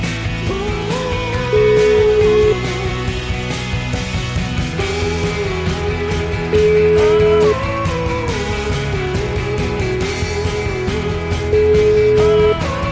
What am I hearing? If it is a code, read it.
Ringback tone (Europe, 425 Hz), with noise.